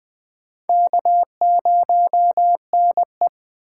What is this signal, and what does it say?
Morse code, heard as a keyed tone, about 20 wpm, 700 Hz. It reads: K0NE
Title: USA Call Signs - 4 Characters - 6x Character Spacing - 20wpm